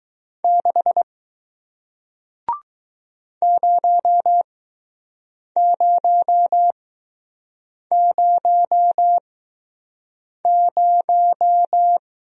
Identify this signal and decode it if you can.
Morse code, heard as a keyed tone, about 20 wpm, spaced as for 6 wpm, 700 Hz. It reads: 6 0000